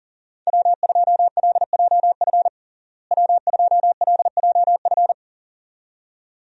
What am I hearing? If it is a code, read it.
Morse code, heard as a keyed tone, about 37 wpm, 700 Hz. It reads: W2LJF W2LJF
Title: USA Call Signs - 5 Characters - Repeated 2x - 40wpm